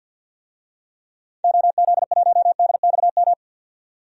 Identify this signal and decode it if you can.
Morse code, heard as a keyed tone, about 45 wpm, 700 Hz. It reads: OZ1DXM
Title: DXCC-100 Call Signs - 6 Characters - 50wpm